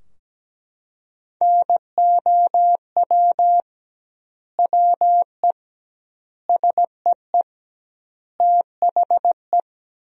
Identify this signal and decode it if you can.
Morse code, heard as a keyed tone, about 17 wpm, 700 Hz. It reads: NOW WE SEE THE
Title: 2x Word-Spacing - Sentences from Top 200 Words - 17wpm